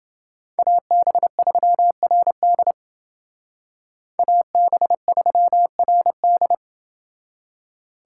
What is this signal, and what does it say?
Morse code, heard as a keyed tone, about 29 wpm, 700 Hz. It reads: AB3RD AB3RD